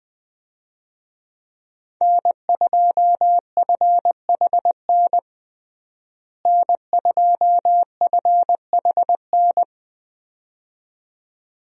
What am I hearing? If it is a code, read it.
Morse code, heard as a keyed tone, about 20 wpm, 700 Hz. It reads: N2FHN N2FHN
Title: USA Call Signs - 5 Characters - Repeated 2x - 20wpm